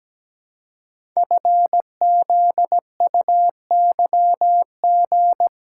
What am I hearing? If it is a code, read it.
Morse code, heard as a keyed tone, about 17 wpm, 700 Hz. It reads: FZUYG
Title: Code Groups - 5 Letter:Numbers - 17wpm